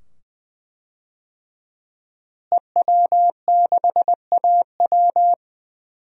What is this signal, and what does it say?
Morse code, heard as a keyed tone, about 20 wpm, 700 Hz. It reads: EW6AW
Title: DXCC-100 Call Signs - 5 Characters - Repeated 2x - 20wpm